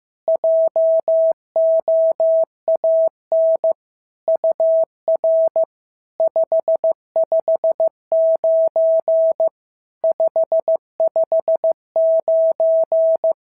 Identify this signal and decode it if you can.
Morse code, heard as a keyed tone, about 15 wpm, 650 Hz. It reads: JOAN UR 559 559
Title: SOTA QSO Practice 15wpm